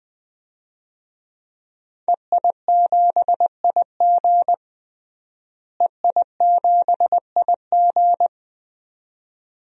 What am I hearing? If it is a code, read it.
Morse code, heard as a keyed tone, about 20 wpm, 700 Hz. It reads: EI7IG EI7IG
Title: DXCC-100 Call Signs - 5 Characters - Repeated 2x - 20wpm